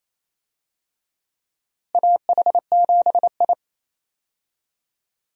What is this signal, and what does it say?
Morse code, heard as a keyed tone, about 28 wpm, 700 Hz. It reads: AH7I
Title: USA Call Signs - 4 Characters - Repeated 3x - 28wpm